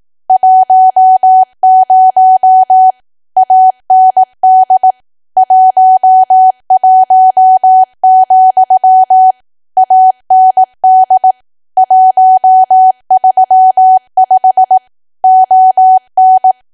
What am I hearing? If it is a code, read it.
Morse code, heard as a keyed tone, about 18 wpm, 750 Hz. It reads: 10 AND 11, AND 135 ON
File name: cwa / W1AW-Level2 / 18WPM